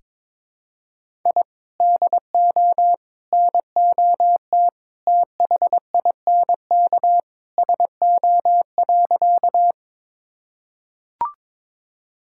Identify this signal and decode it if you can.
Morse code, heard as a keyed tone, about 22 wpm, 700 Hz. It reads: I DO NOT THINK SO.
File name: practice / 22wpm / Sentences with 4-Word Phrases - 22wpm